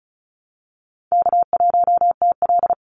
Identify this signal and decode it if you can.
Morse code, heard as a keyed tone, about 34 wpm, 700 Hz. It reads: K1TL
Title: Easy Call Signs Repeated 2x - 35wpm